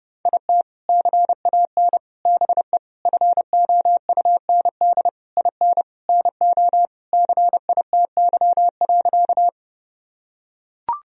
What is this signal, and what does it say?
Morse code, heard as a keyed tone, about 30 wpm, 700 Hz. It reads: IT CAN BE FOUND IN NO CITY.
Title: Sentences from Top 1000 Words - Review Part 1 - 30wpm